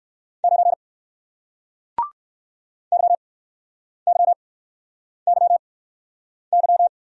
Morse code, heard as a keyed tone, about 54 wpm, spaced as for 9 wpm, 700 Hz. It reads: 9 YYYY
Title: Speed-Racing Single Letter-Number 45wpm